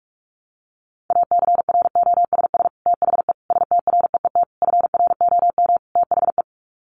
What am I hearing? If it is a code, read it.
Morse code, heard as a keyed tone, about 44 wpm, 700 Hz. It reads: ACROSS THE STREET FROM THE
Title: Sets of 5 Words Encore 45wpm